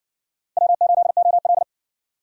Morse code, heard as a keyed tone, about 60 wpm, 690 Hz. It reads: W8OL